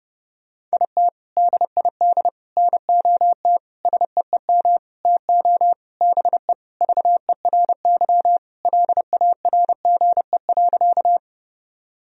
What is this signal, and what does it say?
Morse code, heard as a keyed tone, about 30 wpm, 700 Hz. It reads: IT DID NOT SEEM TO BE VERY LARGE.
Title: Sentences from Top 1000 Words - Review Part 1 - 30wpm